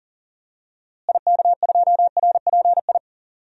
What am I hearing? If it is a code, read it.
Morse code, heard as a keyed tone, about 40 wpm, 700 Hz. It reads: IK2RPI
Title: DXCC-100 Call Signs - 6 Characters - Repeated 2x - 40wpm